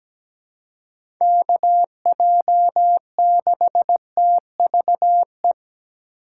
Morse code, heard as a keyed tone, about 17 wpm, 700 Hz. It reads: KJ6TVE
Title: USA Call Signs - 6 Characters - 3x Character Spacing - 17wpm